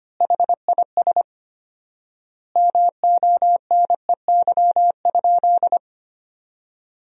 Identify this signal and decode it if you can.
Morse code, heard as a keyed tone, about 25 wpm, 700 Hz. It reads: HIS MONEY?
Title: 4x Word-Spacing - Sentences from Top 400 Words - 25wpm